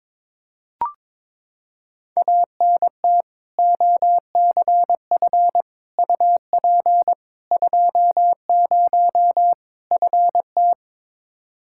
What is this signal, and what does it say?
Morse code, heard as a keyed tone, about 21 wpm, 700 Hz. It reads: ANT OCF UP 20 FT
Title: K1USN SST QSO Practice 22wpm